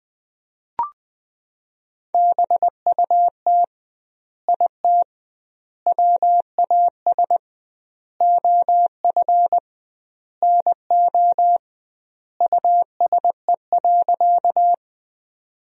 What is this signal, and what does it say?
Morse code, heard as a keyed tone, about 20 wpm, 700 Hz. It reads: BUT IT WAS OF NO USE.
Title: 2x Word-Spacing - Sentences from Top 100 Words - 20wpm